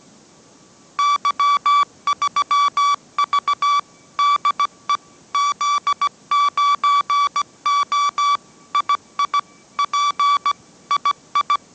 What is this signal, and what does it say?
Morse code, heard as a keyed tone, about 17 wpm, 1150 Hz. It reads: Y3V DE Z9O II P II
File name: mrx / mrx080 18wpm